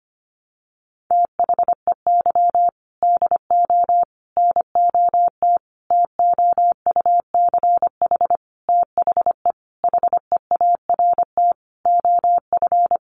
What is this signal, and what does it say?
Morse code, heard as a keyed tone, about 25 wpm, 700 Hz. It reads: THEY DO NOT TOUCH THE HEART OF
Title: Sentences from Top 1200 Words 25wpm